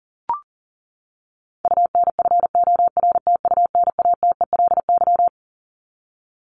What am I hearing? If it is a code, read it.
Morse code, heard as a keyed tone, about 40 wpm, 700 Hz. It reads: UNFORTUNATELY
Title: Top 1751-2000 Words 40wpm